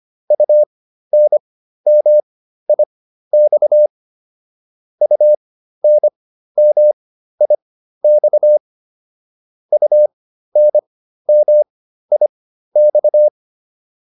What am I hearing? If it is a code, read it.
Morse code, heard as a keyed tone, about 25 wpm, spaced as for 13 wpm, 600 Hz. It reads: UNMIX UNMIX UNMIX
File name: UN5 13